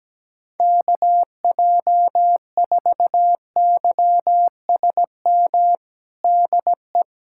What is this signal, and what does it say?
Morse code, heard as a keyed tone, about 17 wpm, 700 Hz. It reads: KJ4YSM DE
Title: USA Callsign DE Callsign - 6 Characters - 17wpm